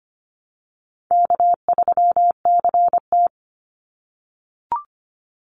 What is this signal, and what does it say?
Morse code, heard as a keyed tone, about 25 wpm, 700 Hz. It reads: K3CT
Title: Easy Call Signs - 25wpm